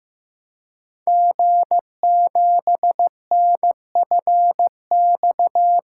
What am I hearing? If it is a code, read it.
Morse code, heard as a keyed tone, about 15 wpm, 700 Hz. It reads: G7NFX
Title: Code Groups - 5 Letter:Numbers - 15wpm